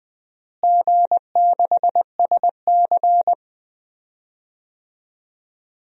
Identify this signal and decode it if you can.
Morse code, heard as a keyed tone, about 20 wpm, 700 Hz. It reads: G6SC
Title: DXCC-100 Call Signs - 4 Characters - Repeated 3x - 20wpm